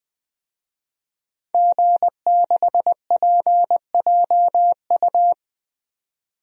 Morse code, heard as a keyed tone, about 20 wpm, 700 Hz. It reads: G6PJU
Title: DXCC-100 Call Signs - 5 Characters - 20wpm